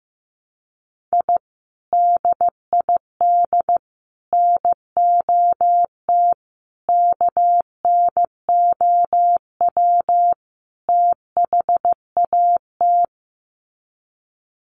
Morse code, heard as a keyed tone, about 15 wpm, 700 Hz. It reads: I DID NOT KNOW THAT
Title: Sets of 5 Words 15wpm